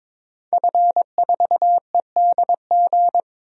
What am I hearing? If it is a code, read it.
Morse code, heard as a keyed tone, about 22 wpm, 700 Hz. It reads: F4EDG
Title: DXCC-100 Call Signs - 5 Characters - 22wpm